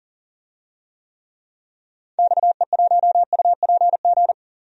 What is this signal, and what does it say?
Morse code, heard as a keyed tone, about 40 wpm, 700 Hz. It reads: XE1UPG
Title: DXCC-100 Call Signs - 6 Characters - 40wpm